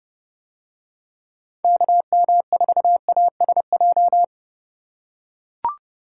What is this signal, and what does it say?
Morse code, heard as a keyed tone, about 31 wpm, 700 Hz. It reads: KM4ASJ